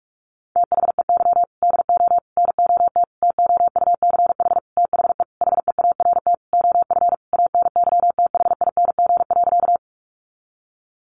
Morse code, heard as a keyed tone, about 44 wpm, 700 Hz. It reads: THEY DO NOT TOUCH THE HEART OF ANYTHING.
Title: Sentences from Top 1200 Words 45wpm